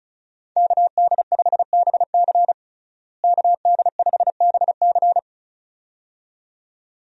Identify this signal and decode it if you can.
Morse code, heard as a keyed tone, about 36 wpm, 700 Hz. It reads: KD5BC KD5BC
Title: USA Call Signs - 5 Characters - Repeated 2x - 35wpm